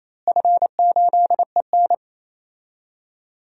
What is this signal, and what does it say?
Morse code, heard as a keyed tone, about 28 wpm, 700 Hz. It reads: F8EN
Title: DXCC-100 Call Signs - 4 Characters - Repeated 2x - 28wpm